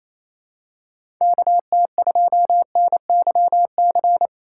Morse code, heard as a keyed tone, about 28 wpm, 700 Hz. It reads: KT2NYC
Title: USA Call Signs - 6 Characters - 2x Character Spacing - 28wpm